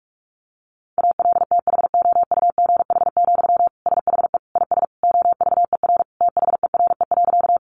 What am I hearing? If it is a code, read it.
Morse code, heard as a keyed tone, about 44 wpm, 700 Hz. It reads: ALTHOUGH, SHE IS OVER THERE.
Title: Sentences from Top 600 Words 45wpm